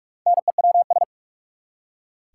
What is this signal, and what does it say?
Morse code, heard as a keyed tone, about 46 wpm, 700 Hz. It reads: NEWS
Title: Four Letter Words - 45wpm